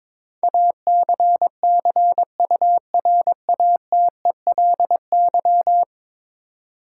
Morse code, heard as a keyed tone, about 22 wpm, 700 Hz. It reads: ACCURATELY
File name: practice / 22wpm / 10 Letter Words - First 3 Letters Given - 22wpm